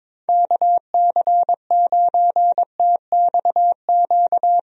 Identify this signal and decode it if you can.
Morse code, heard as a keyed tone, about 22 wpm, 700 Hz. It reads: KC9TXQ
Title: USA Call Signs - 6 Characters - Repeated 2x - 22wpm